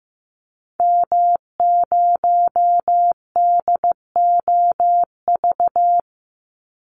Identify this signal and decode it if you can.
Morse code, heard as a keyed tone, about 15 wpm, 700 Hz. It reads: M0DOV
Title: Call Signs Repeated 2x - 15wpm